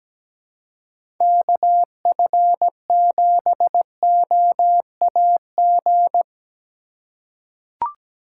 Morse code, heard as a keyed tone, about 17 wpm, 700 Hz. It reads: KF7OAG
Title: USA Call Signs - 6 Characters - 2x Character Spacing - 17wpm